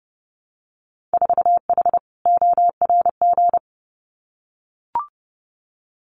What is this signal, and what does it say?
Morse code, heard as a keyed tone, about 30 wpm, 700 Hz. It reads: 4H ORG
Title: Winter Field Day Exchanges Encore 30wpm